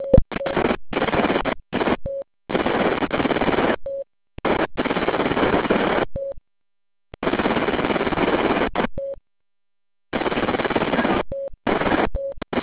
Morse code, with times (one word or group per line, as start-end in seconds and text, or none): none